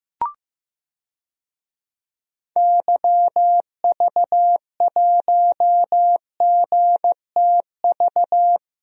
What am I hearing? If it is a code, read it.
Morse code, heard as a keyed tone, about 15 wpm, 700 Hz. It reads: YV1GTV
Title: DXCC-100 Call Signs - 6 Characters - Repeated 2x - 15wpm